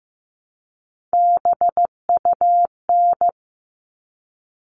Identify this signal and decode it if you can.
Morse code, heard as a keyed tone, about 15 wpm, 700 Hz. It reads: BUN
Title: All Cut Numbers - 3-Digit 15wpm